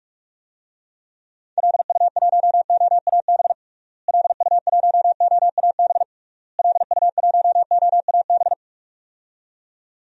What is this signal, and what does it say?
Morse code, heard as a keyed tone, about 41 wpm, 700 Hz. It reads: LU1OAB LU1OAB LU1OAB
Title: DXCC-100 Call Signs - 6 Characters - Repeated 3x - 45wpm